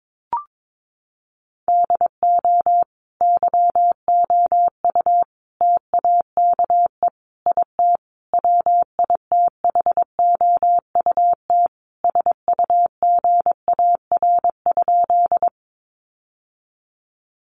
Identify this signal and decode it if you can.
Morse code, heard as a keyed tone, about 22 wpm, 700 Hz. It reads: DO YOU TAKE IT WITHOUT SUGAR?
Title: Sentences from Top 1200 Words 22wpm